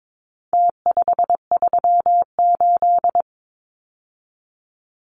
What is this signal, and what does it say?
Morse code, heard as a keyed tone, about 22 wpm, 700 Hz. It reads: T538
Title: Numbers - Contest Style - 4-Digit 22wpm